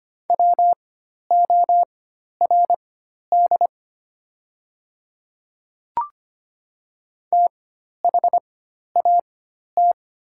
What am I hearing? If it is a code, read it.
Morse code, heard as a keyed tone, about 25 wpm, spaced as for 12 wpm, 700 Hz. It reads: WORD THAT
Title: 4x Character Spacing - Top 200 Words - 25wpm